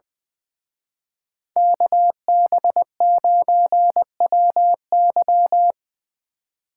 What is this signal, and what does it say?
Morse code, heard as a keyed tone, about 24 wpm, 700 Hz. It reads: KB9WY
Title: Speed Racing - USA 2x2 Call Signs 20wpm